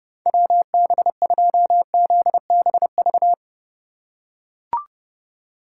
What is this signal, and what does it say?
Morse code, heard as a keyed tone, about 30 wpm, 700 Hz. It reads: WB2ZBV
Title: USA Call Signs - 6 Characters - 3x Character Spacing - 30wpm